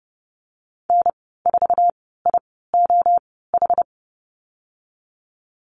Morse code, heard as a keyed tone, about 30 wpm, spaced as for 17 wpm, 700 Hz. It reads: N4IOH
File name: Lesson 14g - Number 2 - Call Signs 30wpm